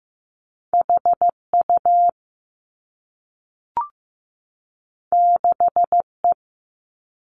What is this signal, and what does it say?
Morse code, heard as a keyed tone, about 15 wpm, 700 Hz. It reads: H U 6 E
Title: Two Challenging Letter-Number Combinations 15wpm